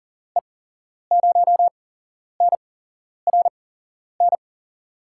Morse code, heard as a keyed tone, about 40 wpm, 700 Hz. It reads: E 0 N R N
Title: USA Call Signs - 6 Characters - 8x Character Spacing - 40wpm